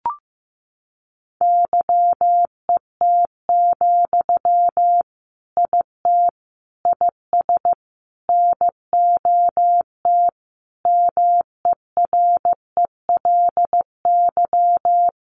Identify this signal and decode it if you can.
Morse code, heard as a keyed tone, about 15 wpm, 700 Hz. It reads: YET, IT IS NOT MERELY
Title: Sentences from Top 1400 Words 15wpm